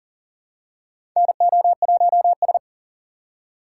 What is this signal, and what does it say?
Morse code, heard as a keyed tone, about 40 wpm, 700 Hz. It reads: NO1S
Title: USA Call Signs - 4 Characters - 3x Character Spacing - 40wpm